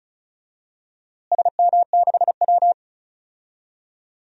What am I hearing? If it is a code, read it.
Morse code, heard as a keyed tone, about 34 wpm, 700 Hz. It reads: SM6W
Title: DXCC-100 Call Signs - 4 Characters - Repeated 3x - 35wpm